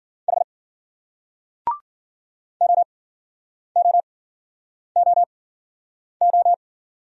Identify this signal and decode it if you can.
Morse code, heard as a keyed tone, about 50 wpm, spaced as for 9 wpm, 700 Hz. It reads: H OOOO